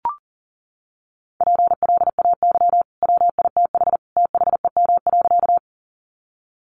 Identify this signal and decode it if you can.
Morse code, heard as a keyed tone, about 40 wpm, 700 Hz. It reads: PLAY WITH THEM.
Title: Sentences from Top 200 Words 40wpm